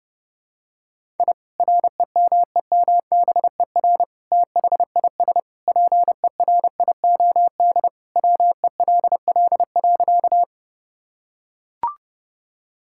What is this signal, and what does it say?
Morse code, heard as a keyed tone, about 32 wpm, 700 Hz. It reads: I REMEMBER THIS PERIOD WELL.
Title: Sentences from Top 1000 Words - Review Part 5 - 30wpm